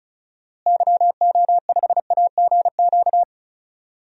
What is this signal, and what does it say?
Morse code, heard as a keyed tone, about 36 wpm, 700 Hz. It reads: YO5AGQ
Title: DXCC-100 Call Signs - 6 Characters - Repeated 2x - 35wpm